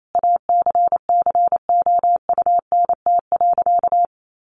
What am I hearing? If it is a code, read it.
Morse code, heard as a keyed tone, about 28 wpm, 700 Hz. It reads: ACCOUNT.
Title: Sentences from Top 1300 Words 28wpm